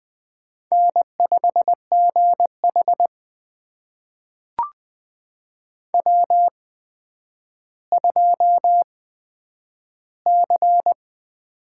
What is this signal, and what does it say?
Morse code, heard as a keyed tone, about 20 wpm, spaced as for 10 wpm, 700 Hz. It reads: N5GH W2C
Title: USA Call Signs - 4 Characters - 8x Character Spacing - 20wpm